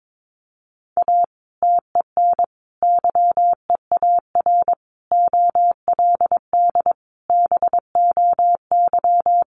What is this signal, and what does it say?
Morse code, heard as a keyed tone, about 22 wpm, 700 Hz. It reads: A TEN YEAR OLD BOY
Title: Sentences from Top 1300 Words 22wpm